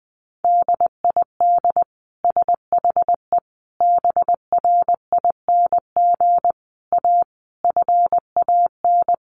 Morse code, heard as a keyed tone, about 20 wpm, 700 Hz. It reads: DID SHE BRING A FAN
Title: Sentences from Top 1300 Words 20wpm